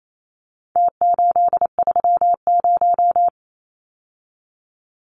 Morse code, heard as a keyed tone, about 28 wpm, 700 Hz. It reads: T830
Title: Numbers - Contest Style - 4-Digit 28wpm